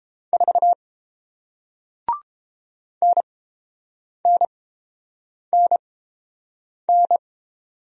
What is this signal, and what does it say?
Morse code, heard as a keyed tone, about 30 wpm, spaced as for 7 wpm, 700 Hz. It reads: 4 NNNN